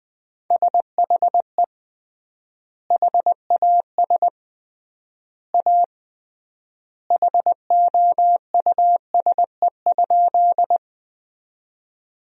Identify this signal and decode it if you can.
Morse code, heard as a keyed tone, about 21 wpm, 700 Hz. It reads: SHE HAS A HOUSE?
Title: 3x Word-Spacing - Sentences from Top 200 Words - 20wpm